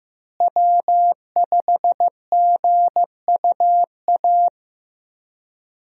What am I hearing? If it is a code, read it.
Morse code, heard as a keyed tone, about 15 wpm, 700 Hz. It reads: W5GUA
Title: USA Call Signs - 5 Characters - Repeated 2x - 15wpm